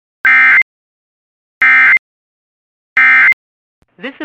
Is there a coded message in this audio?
Yes: Emergency Alert System (SAME) tones from 0.2 to 3.3 s, heard as harsh buzzing bursts, with speech.